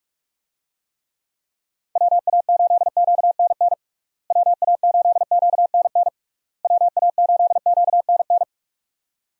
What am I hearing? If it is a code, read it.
Morse code, heard as a keyed tone, about 46 wpm, 700 Hz. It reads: WA8QNN WA8QNN WA8QNN
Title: USA Call Signs - 6 Characters - Repeated 3x - 45wpm